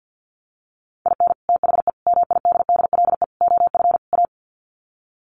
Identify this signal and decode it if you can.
Morse code, heard as a keyed tone, about 50 wpm, 700 Hz. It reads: IN THE MIDDLE OF A
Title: Sets of 5 Words 50wpm